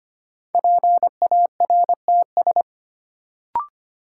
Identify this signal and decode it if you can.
Morse code, heard as a keyed tone, about 25 wpm, 700 Hz. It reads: PARTS
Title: Sentences from Top 1000 Words - Review Part 6 - New Word Drill - 25wpm